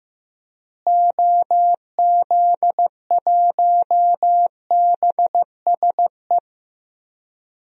Morse code, heard as a keyed tone, about 15 wpm, 700 Hz. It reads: OZ1BSE
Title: DXCC-100 Call Signs - 6 Characters - Repeated 3x - 15wpm